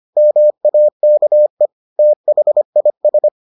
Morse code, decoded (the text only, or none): MAKE THIS